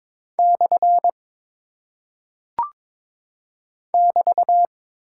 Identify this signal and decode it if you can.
Morse code, heard as a keyed tone, about 22 wpm, 700 Hz. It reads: / <BT>
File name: practice / 22wpm / Single Punctuation-Prosign - 22wpm